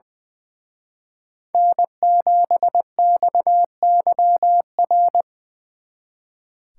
Morse code, heard as a keyed tone, about 20 wpm, 700 Hz. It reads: N7XYR
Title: Speed Racing - USA Call Signs 20wpm